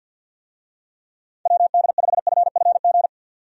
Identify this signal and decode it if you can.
Morse code, heard as a keyed tone, about 46 wpm, 700 Hz. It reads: WD5FFG
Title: USA Call Signs - 6 Characters - 50wpm